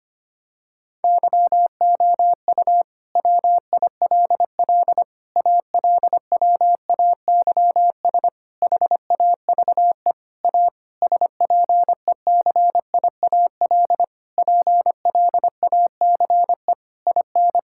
Morse code, heard as a keyed tone, about 25 wpm, 700 Hz. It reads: YOU WILL ALWAYS HAVE A SPECIAL PLACE IN
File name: practice / 25wpm / Sentences with 9-Word Phrases - 25wpm